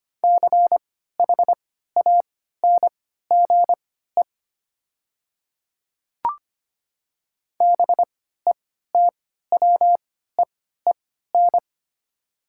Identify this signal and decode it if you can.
Morse code, heard as a keyed tone, about 25 wpm, spaced as for 14 wpm, 700 Hz. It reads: CHANGE BETWEEN
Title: 3x Character Spacing - Top 300 Words - 25wpm